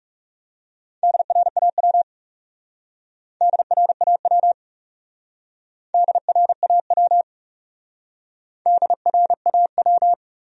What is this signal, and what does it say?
Morse code, heard as a keyed tone, about 40 wpm, 700 Hz. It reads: DRAW DRAW DRAW DRAW